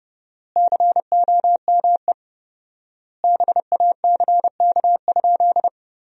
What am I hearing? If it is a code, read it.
Morse code, heard as a keyed tone, about 30 wpm, 700 Hz. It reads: COME BACK?
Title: 4x Word-Spacing - Sentences from Top 300 Words - 30wpm